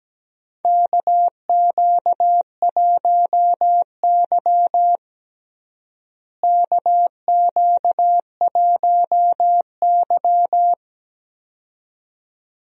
Morse code, heard as a keyed tone, about 17 wpm, 700 Hz. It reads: KQ1Y KQ1Y
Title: USA Call Signs - 4 Characters - Repeated 2x - 17wpm